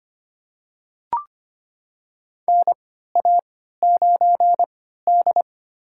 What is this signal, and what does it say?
Morse code, heard as a keyed tone, about 25 wpm, spaced as for 14 wpm, 700 Hz. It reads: NA9D
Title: USA Call Signs - 4 Characters - 3x Character Spacing - 25wpm